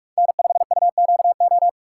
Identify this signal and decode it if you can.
Morse code, heard as a keyed tone, about 46 wpm, 700 Hz. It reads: N5UQO
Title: USA Call Signs - 2x Character Spacing - 45wpm